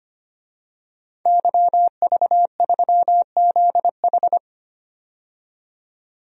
Morse code, heard as a keyed tone, about 25 wpm, 700 Hz. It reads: YV3ZH
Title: DXCC-100 Call Signs - 5 Characters - 25wpm